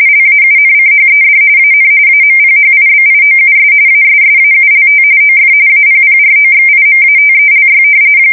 RTTY, about 45.45 baud. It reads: LAUNCHED AN AMPHIBIOUS ASSAULT OF APPROXIMATELY
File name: fsk170 / usnavy3